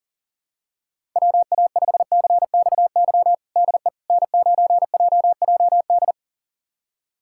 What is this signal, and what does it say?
Morse code, heard as a keyed tone, about 37 wpm, 700 Hz. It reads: WA5CXY DE N9JJD
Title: USA Callsign DE Callsign 40wpm